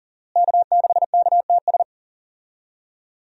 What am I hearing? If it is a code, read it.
Morse code, heard as a keyed tone, about 40 wpm, 700 Hz. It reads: K6KTS
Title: USA Call Signs - 5 Characters - Repeated 2x - 40wpm